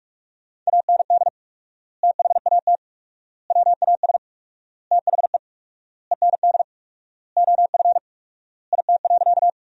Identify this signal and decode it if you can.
Morse code, heard as a keyed tone, about 46 wpm, 700 Hz. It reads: AND THAT WAS THE END OF IT.
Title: 4x Word-Spacing - Sentences from Top 200 Words - 45wpm